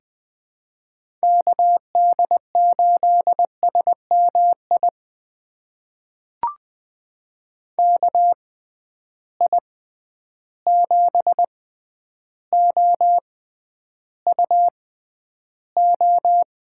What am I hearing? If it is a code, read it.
Morse code, heard as a keyed tone, about 20 wpm, spaced as for 10 wpm, 700 Hz. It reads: KD8SMI KI7OUO
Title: USA Call Signs - 6 Characters - 6x Character Spacing - 20wpm